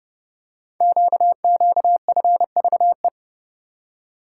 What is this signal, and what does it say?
Morse code, heard as a keyed tone, about 28 wpm, 700 Hz. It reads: QQFVE
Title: Code Groups - 5 Letter:Numbers - 30wpm